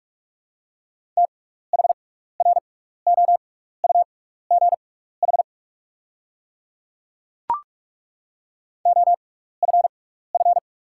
Morse code, heard as a keyed tone, about 43 wpm, spaced as for 15 wpm, 700 Hz. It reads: THROUGH OFF
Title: 6x Character Spacing - Top 200 Words - 45wpm